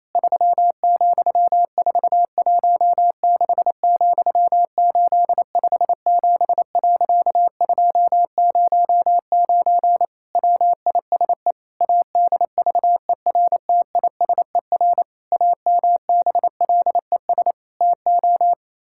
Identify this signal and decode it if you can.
Morse code, heard as a keyed tone, about 28 wpm, 700 Hz. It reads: 3,416,857.209 WISE ADVERTISER AMBLES TO
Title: Pangram Challenge v2 - 28wpm